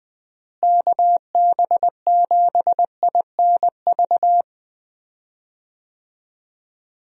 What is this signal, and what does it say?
Morse code, heard as a keyed tone, about 20 wpm, 700 Hz. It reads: KB7INV
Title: USA Call Signs - 6 Characters - Repeated 2x - 20wpm